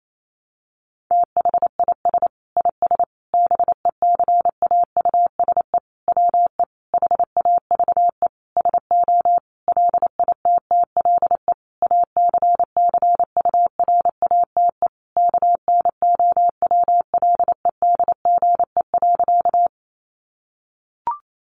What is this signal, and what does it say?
Morse code, heard as a keyed tone, about 28 wpm, 700 Hz. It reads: THIS IS BECAUSE WE HAVE SO LITTLE ACCURATE KNOWLEDGE.
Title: Sentences from Top 1200 Words 28wpm